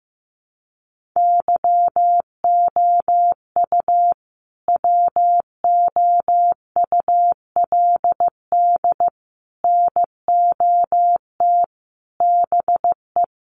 Morse code, heard as a keyed tone, about 15 wpm, 700 Hz. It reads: YOU WOULD NOT BE
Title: Sentences from Top 1200 Words 15wpm